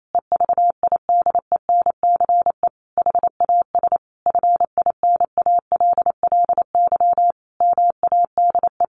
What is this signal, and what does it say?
Morse code, heard as a keyed tone, about 28 wpm, 700 Hz. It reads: EVIDENCE HAS FINALLY MADE
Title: Sentences from Top 1300 Words 28wpm